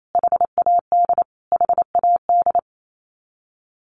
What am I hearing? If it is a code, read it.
Morse code, heard as a keyed tone, about 28 wpm, 700 Hz. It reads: HAD HAD
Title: Top 500 Words Repeated 28wpm